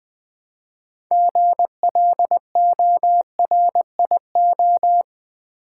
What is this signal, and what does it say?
Morse code, heard as a keyed tone, about 20 wpm, 700 Hz. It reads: GLORIO
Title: Build A Word - 2 Letters At A Time - 8-Letter Words - 20wpm